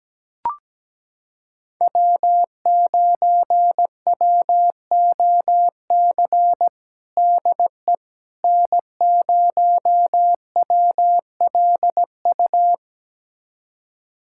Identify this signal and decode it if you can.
Morse code, heard as a keyed tone, about 17 wpm, 700 Hz. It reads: W9WOC DE N0WLU
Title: USA Callsign DE Callsign - 5 Characters - 17wpm